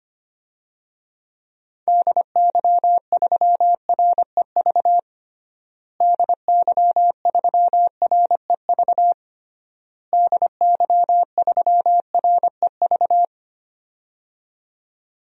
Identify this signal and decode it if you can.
Morse code, heard as a keyed tone, about 25 wpm, 700 Hz. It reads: DY3REV DY3REV DY3REV
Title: DXCC-100 Call Signs - 6 Characters - Repeated 3x - 25wpm